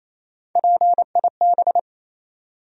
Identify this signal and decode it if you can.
Morse code, heard as a keyed tone, about 27 wpm, 700 Hz. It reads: PIB
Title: USA Call Sign Postfixes - 3 Letters 28wpm